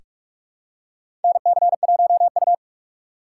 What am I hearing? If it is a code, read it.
Morse code, heard as a keyed tone, about 46 wpm, 700 Hz. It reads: NC1U
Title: USA Call Signs - 4 Characters - 2x Character Spacing - 45wpm